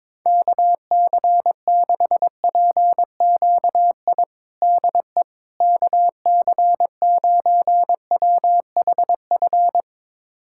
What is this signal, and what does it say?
Morse code, heard as a keyed tone, about 22 wpm, 700 Hz. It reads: KC6PQI DE KC9WHF
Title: USA Callsign DE Callsign - 6 Characters - 22wpm